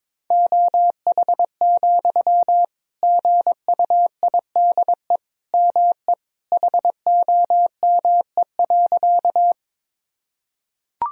Morse code, heard as a keyed tone, about 22 wpm, 700 Hz. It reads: OH, GUIDE ME HOME.
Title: Sentences from Top 1000 Words - Review Part 3 - 22wpm